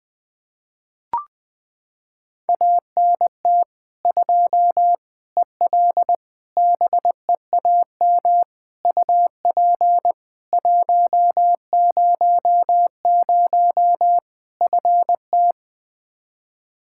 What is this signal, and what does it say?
Morse code, heard as a keyed tone, about 20 wpm, 700 Hz. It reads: ANT 2 EL BEAM UP 100 FT
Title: Practice QSO - Exchange 2 - Info - 20wpm